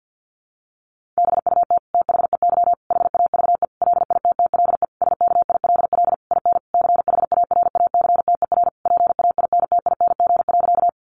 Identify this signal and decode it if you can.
Morse code, heard as a keyed tone, about 50 wpm, 700 Hz. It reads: BUT THEY HAVE LITTLE SKILL IN CHARACTER PAINTING.
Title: Sentences from Top 1200 Words 50wpm